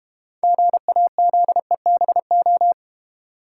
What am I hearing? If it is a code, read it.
Morse code, heard as a keyed tone, about 29 wpm, 700 Hz. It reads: GAZEBO
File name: practice / 25wpm / Difficult Words - Speed-Racing with Hints - 25wpm